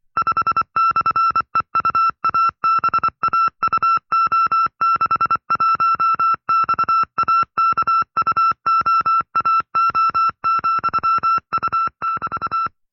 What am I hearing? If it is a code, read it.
Morse code, heard as a keyed tone, about 24 wpm, 1340 Hz. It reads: 5/EUABAUO61XAKUOAO,U<BT>
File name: FreeFiles / CW-practice / 20WPM-side2